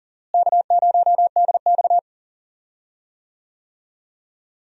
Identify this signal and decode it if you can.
Morse code, heard as a keyed tone, about 40 wpm, 700 Hz. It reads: K0DX
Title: USA Call Signs - 4 Characters - Repeated 2x - 40wpm